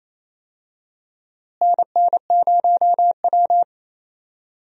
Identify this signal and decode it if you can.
Morse code, heard as a keyed tone, about 28 wpm, 700 Hz. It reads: NN0W
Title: USA Call Signs - 4 Characters - Repeated 3x - 28wpm